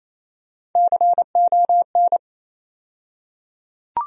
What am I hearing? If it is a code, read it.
Morse code, heard as a keyed tone, about 27 wpm, 700 Hz. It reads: CON
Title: USA Call Sign Postfixes - 3 Letters 28wpm